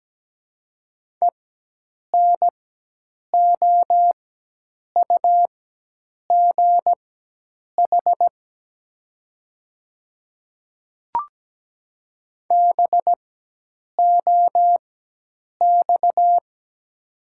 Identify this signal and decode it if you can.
Morse code, heard as a keyed tone, about 17 wpm, spaced as for 8 wpm, 700 Hz. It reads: ENOUGH BOX
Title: 4x Character Spacing - Top 500 Words - 17wpm